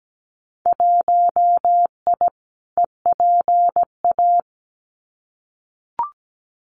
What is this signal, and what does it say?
Morse code, heard as a keyed tone, about 17 wpm, 700 Hz. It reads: 1I EPA
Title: Winter Field Day Exchanges Encore 17wpm